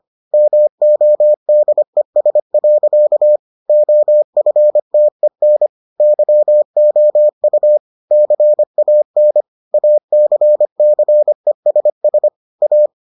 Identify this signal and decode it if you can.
Morse code, heard as a keyed tone, about 25 wpm, 600 Hz. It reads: MODES. OFTEN YOU CAN ACCESS A